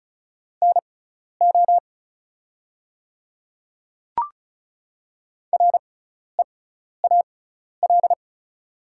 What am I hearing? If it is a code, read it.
Morse code, heard as a keyed tone, about 36 wpm, spaced as for 12 wpm, 700 Hz. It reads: NO REAL
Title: 6x Character Spacing - Top 300 Words - 35wpm